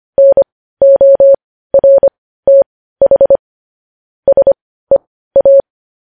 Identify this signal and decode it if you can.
Morse code, heard as a keyed tone, about 25 wpm, spaced as for 15 wpm, 550 Hz. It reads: NORTH SEA
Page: news headlines in morse code at 15 wpm